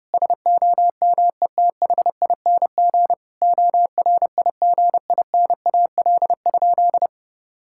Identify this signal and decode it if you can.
Morse code, heard as a keyed tone, about 30 wpm, 700 Hz. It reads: SOMETHING ORIGINAL?
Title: Sentences from Top 1000 Words - Review Part 6 - 30wpm